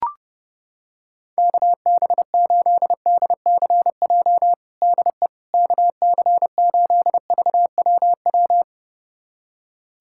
Morse code, heard as a keyed tone, about 30 wpm, 700 Hz. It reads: KB8DCJ DE KC8VWW
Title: USA Callsign DE Callsign - 6 Characters - 30wpm